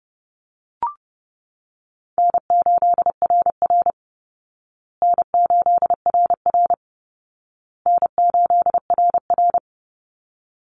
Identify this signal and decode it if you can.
Morse code, heard as a keyed tone, about 28 wpm, 700 Hz. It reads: N8RR N8RR N8RR
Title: Easy Call Signs Repeated 3x - 30wpm